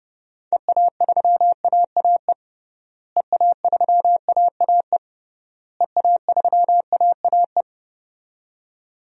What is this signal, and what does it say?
Morse code, heard as a keyed tone, about 30 wpm, 700 Hz. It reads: EA3AAE EA3AAE EA3AAE
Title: DXCC-100 Call Signs - 6 Characters - Repeated 3x - 30wpm